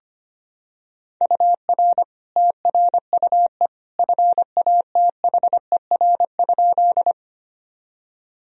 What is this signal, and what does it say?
Morse code, heard as a keyed tone, about 25 wpm, 700 Hz. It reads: UR TRUE FATHER?
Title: Sentences from Top 400 Words and Abbreviations 25wpm